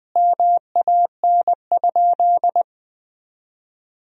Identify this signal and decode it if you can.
Morse code, heard as a keyed tone, about 20 wpm, 700 Hz. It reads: MAN?
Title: 2x Word-Spacing - Sentences from Top 200 Words - 20wpm